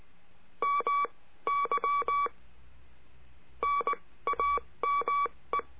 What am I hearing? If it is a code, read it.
Morse code, heard as a keyed tone, about 19 wpm, spaced as for 15 wpm, 1140 Hz. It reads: MY NAME
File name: cursos / CURSO DE TELEGRAFIA / Telegrafia - Trilha 22